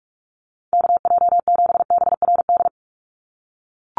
Morse code, heard as a keyed tone, about 44 wpm, 700 Hz. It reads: KJ7BRD
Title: USA Call Signs 45wpm